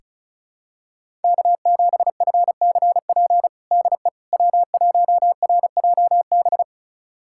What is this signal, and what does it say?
Morse code, heard as a keyed tone, about 36 wpm, 700 Hz. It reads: K7FCP DE W1RJB
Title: USA Callsign DE Callsign - 5 Characters - 35wpm